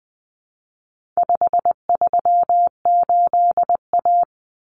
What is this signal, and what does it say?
Morse code, heard as a keyed tone, about 20 wpm, 700 Hz. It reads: 538A
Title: Common Cut Numbers - 4-Digit 20wpm